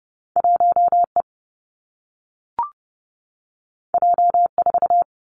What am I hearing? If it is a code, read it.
Morse code, heard as a keyed tone, about 32 wpm, 700 Hz. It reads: 1E J4
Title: Two Challenging Letter-Number Combinations 30wpm